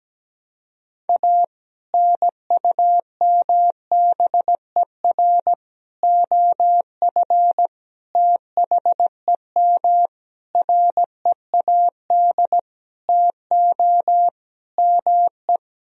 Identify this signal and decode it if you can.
Morse code, heard as a keyed tone, about 17 wpm, 700 Hz. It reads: A NUMBER OF THEM READ TO ME
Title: Sentences from Top 1000 Words - Review Part 1 - 17wpm